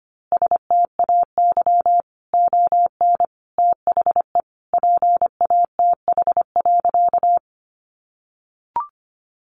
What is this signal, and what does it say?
Morse code, heard as a keyed tone, about 25 wpm, 700 Hz. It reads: STAY ON THE PATH.
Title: A Sound Of Thunder 25wpm